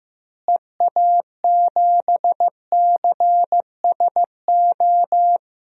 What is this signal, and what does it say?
Morse code, heard as a keyed tone, about 15 wpm, 700 Hz. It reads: EA7CSO
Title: DXCC-100 Call Signs - 6 Characters - Repeated 3x - 15wpm